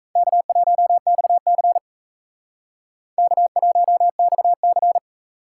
Morse code, heard as a keyed tone, about 35 wpm, 700 Hz. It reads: K1XC K1XC